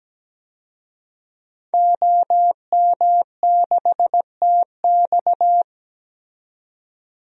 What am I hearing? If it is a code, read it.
Morse code, heard as a keyed tone, about 17 wpm, 700 Hz. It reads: OM6TX
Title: DXCC-100 Call Signs - 5 Characters - 17wpm